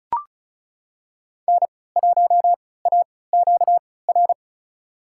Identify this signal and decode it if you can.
Morse code, heard as a keyed tone, about 36 wpm, spaced as for 20 wpm, 700 Hz. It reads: N1AQR
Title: USA Call Signs - 5 Characters - 3x Character Spacing - 35wpm